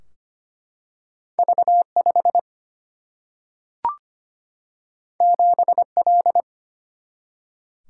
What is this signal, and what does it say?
Morse code, heard as a keyed tone, about 25 wpm, 700 Hz. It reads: V5 7L
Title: Single-Letter Single-Number 25wpm